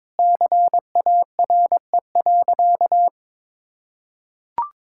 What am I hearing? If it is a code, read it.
Morse code, heard as a keyed tone, about 22 wpm, 700 Hz. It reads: CARE.
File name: Sentences with 9-Word Phrases - 22wpm